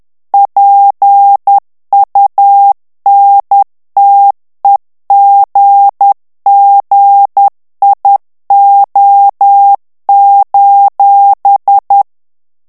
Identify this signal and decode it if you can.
Morse code, heard as a keyed tone, about 10 wpm, 800 Hz. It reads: PUNTEGGIO:
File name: ITALIEN texte clair 2 vitesse 10 mots minute